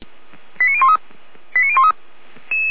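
Five-tone selective call (ZVEI), heard as quick runs of five tones, tones of about 70 ms. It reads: 89013 89013
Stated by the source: ZVEI five-tone selective call received on an SDR.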